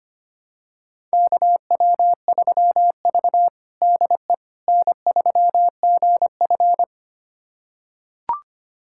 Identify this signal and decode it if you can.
Morse code, heard as a keyed tone, about 25 wpm, 700 Hz. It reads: KW3V DE N3GF
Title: USA Callsign DE Callsign - 4 Characters - 25wpm